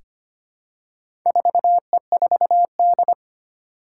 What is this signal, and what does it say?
Morse code, heard as a keyed tone, about 25 wpm, 700 Hz. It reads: 4E4D
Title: Four Challenging Letter-Number Combinations 25wpm